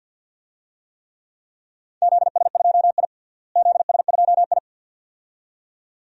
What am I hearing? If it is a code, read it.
Morse code, heard as a keyed tone, about 48 wpm, 700 Hz. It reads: ZS2I ZS2I
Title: DXCC-100 Call Signs - 4 Characters - Repeated 2x - 50wpm